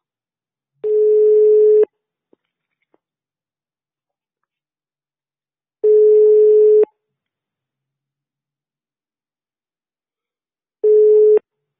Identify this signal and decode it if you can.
Ringback tone (Europe, 425 Hz).